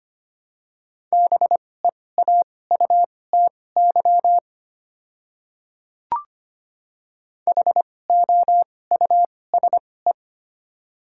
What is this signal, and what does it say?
Morse code, heard as a keyed tone, about 25 wpm, spaced as for 18 wpm, 700 Hz. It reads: BEAUTY HOUSE
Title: 2x Character Spacing - Top 500 Words - 25wpm